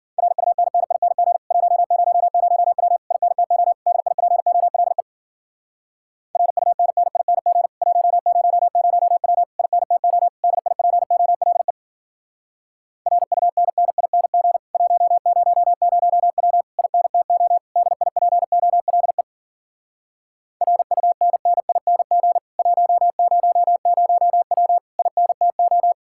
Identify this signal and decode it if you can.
Morse code, heard as a keyed tone, about 52 wpm, 700 Hz. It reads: RUNNING 100W INTO DIPOLE RUNNING 100W INTO DIPOLE RUNNING 100W INTO DIPOLE RUNNING 100W INTO